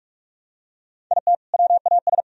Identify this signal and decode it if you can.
Morse code, heard as a keyed tone, about 47 wpm, 700 Hz. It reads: IT WAS
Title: Sentences from Top 1000 Words - Review Part 1 - 45wpm